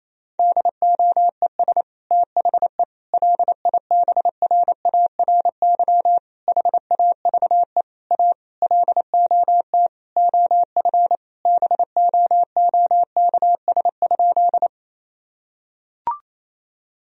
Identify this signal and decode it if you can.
Morse code, heard as a keyed tone, about 27 wpm, 700 Hz. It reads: DOES THE LIBRARY HAVE A LOT OF BOOKS?
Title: Sentences from Top 1000 Words - Review Part 4 - 28wpm